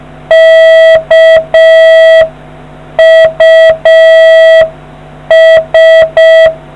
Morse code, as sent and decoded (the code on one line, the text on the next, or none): -.- ..- ...
KUS